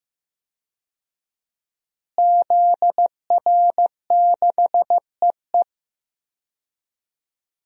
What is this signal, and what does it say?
Morse code, heard as a keyed tone, about 15 wpm, 700 Hz. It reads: ZR6EE
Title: DXCC-100 Call Signs - 5 Characters - Repeated 2x - 15wpm